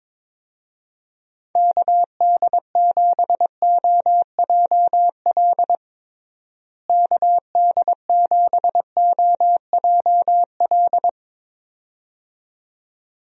Morse code, heard as a keyed tone, about 22 wpm, 700 Hz. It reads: KD7OJL KD7OJL
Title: USA Call Signs - 6 Characters - Repeated 2x - 22wpm